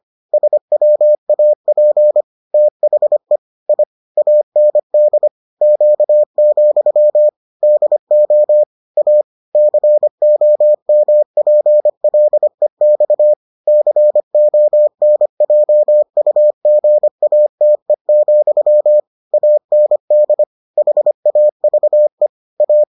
Morse code, heard as a keyed tone, about 25 wpm, 600 Hz. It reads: SWAP THE I AND Q, DO A COMPLEX CONJUGATE, AND HAVE A